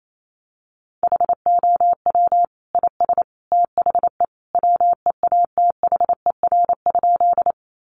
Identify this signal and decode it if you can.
Morse code, heard as a keyed tone, about 28 wpm, 700 Hz. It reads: HOW IS THE WEATHER?
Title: Sentences from Top 800 Words 28wpm